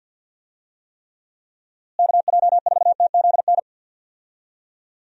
Morse code, heard as a keyed tone, about 50 wpm, 700 Hz. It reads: KJ4TZN
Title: USA Call Signs - 6 Characters - Repeated 2x - 50wpm